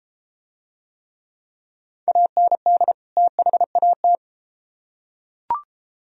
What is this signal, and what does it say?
Morse code, heard as a keyed tone, about 33 wpm, 700 Hz. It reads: AND THAT